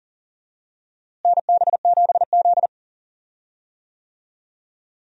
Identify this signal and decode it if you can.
Morse code, heard as a keyed tone, about 40 wpm, 700 Hz. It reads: NB7Z
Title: USA Call Signs - 4 Characters - Repeated 2x - 40wpm